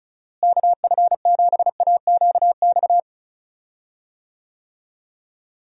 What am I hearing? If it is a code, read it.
Morse code, heard as a keyed tone, about 34 wpm, 700 Hz. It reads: KF7AQX
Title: USA Call Signs - 6 Characters - Repeated 3x - 35wpm